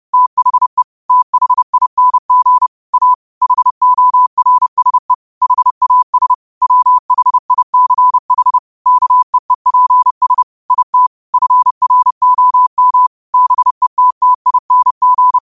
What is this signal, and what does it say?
Morse code, heard as a keyed tone, about 30 wpm, 1000 Hz. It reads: THE THING A HORSE HAS WHICH KEEPS IT FROM BETTING
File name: qotdfiles / 30wpm